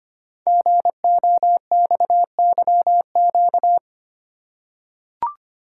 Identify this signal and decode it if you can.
Morse code, heard as a keyed tone, about 25 wpm, 700 Hz. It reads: GOXYQ
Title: Code Groups - 5 Letter:Numbers - 25wpm